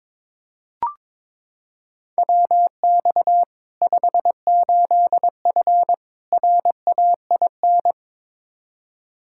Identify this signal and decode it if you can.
Morse code, heard as a keyed tone, about 22 wpm, 700 Hz. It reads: WX 58F RAIN
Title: Practice QSO - Exchange 2 - Info - 22wpm